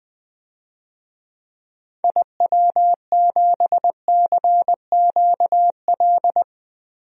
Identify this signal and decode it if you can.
Morse code, heard as a keyed tone, about 21 wpm, 700 Hz. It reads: IW7CQL
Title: DXCC-100 Call Signs - 6 Characters - 20wpm